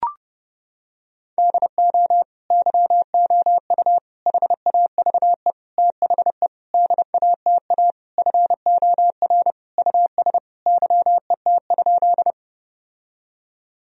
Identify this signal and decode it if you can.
Morse code, heard as a keyed tone, about 30 wpm, 700 Hz. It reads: DO YOU HAVE THE DATA FOR US YET?
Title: Sentences from Top 1000 Words - Review Part 3 - 30wpm